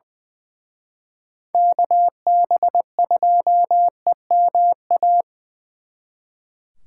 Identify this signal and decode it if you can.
Morse code, heard as a keyed tone, about 24 wpm, 700 Hz. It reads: KB2EMA
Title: Speed Racing - USA 2x3 Call Signs 20wpm